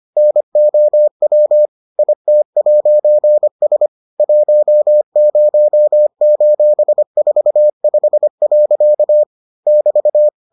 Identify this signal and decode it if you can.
Morse code, heard as a keyed tone, about 25 wpm, 600 Hz. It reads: NOW IT'S 10:45. <BT>